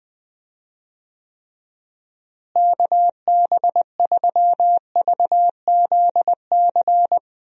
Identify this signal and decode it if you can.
Morse code, heard as a keyed tone, about 20 wpm, 700 Hz. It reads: KB3VZC
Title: USA Call Signs - 6 Characters - Repeated 3x - 20wpm